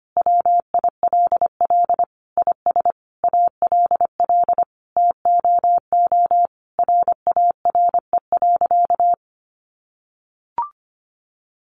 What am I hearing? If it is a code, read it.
Morse code, heard as a keyed tone, about 25 wpm, 700 Hz. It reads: WILL IS ALL TOO RARE.
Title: Sentences from Top 1500 Words 25wpm